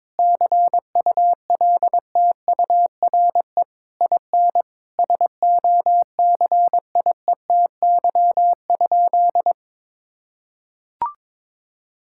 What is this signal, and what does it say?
Morse code, heard as a keyed tone, about 22 wpm, 700 Hz. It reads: CULTURE IN SOCIETY?
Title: Sentences from Top 1000 Words - Review Part 6 - 22wpm